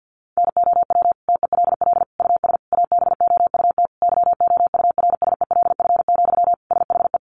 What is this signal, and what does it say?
Morse code, heard as a keyed tone, about 50 wpm, 700 Hz. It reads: NOW TELL US ABOUT YOURSELF, SHE
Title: Sentences from Top 600 Words 50wpm